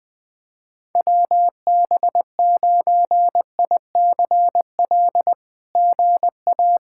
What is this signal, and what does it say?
Morse code, heard as a keyed tone, about 20 wpm, 700 Hz. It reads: WB9ICL GA
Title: SOTA QSO Practice 20wpm